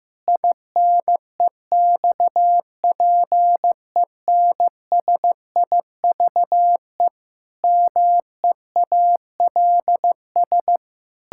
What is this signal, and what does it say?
Morse code, heard as a keyed tone, about 15 wpm, 700 Hz. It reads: INEXPENSIVE MEALS
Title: Pangram Challenge v2 - 15wpm